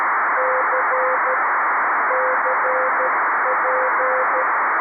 Morse code, heard as a keyed tone, about 13 wpm, 520 Hz, with radio noise. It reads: C CP
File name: arissat / recordings / ARISSat 11Sept2011 1354utc 60frames